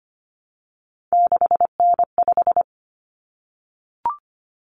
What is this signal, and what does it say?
Morse code, heard as a keyed tone, about 25 wpm, 700 Hz. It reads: 6N5
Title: Common Cut Numbers - 3-Digit 25wpm